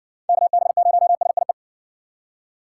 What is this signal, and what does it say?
Morse code, heard as a keyed tone, about 60 wpm, 700 Hz. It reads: KB0SIE